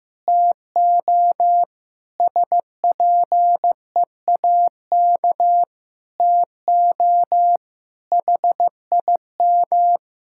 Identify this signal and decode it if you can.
Morse code, heard as a keyed tone, about 15 wpm, 700 Hz. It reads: TO SPEAK TO HIM
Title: Sentences from Top 1600 Words 15wpm